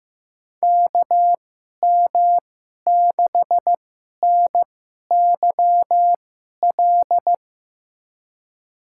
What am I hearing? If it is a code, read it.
Morse code, heard as a keyed tone, about 15 wpm, spaced as for 11 wpm, 700 Hz. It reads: KM6NYL
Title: USA Call Signs - 6 Characters - 2x Character Spacing - 15wpm